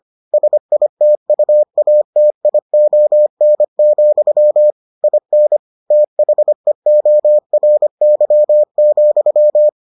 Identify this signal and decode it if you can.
Morse code, heard as a keyed tone, about 25 wpm, 600 Hz. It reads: SITUATION, IN THEORY,